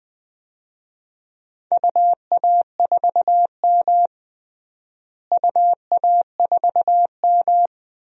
Morse code, heard as a keyed tone, about 20 wpm, 700 Hz. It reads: UA4M UA4M
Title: DXCC-100 Call Signs - 4 Characters - Repeated 2x - 20wpm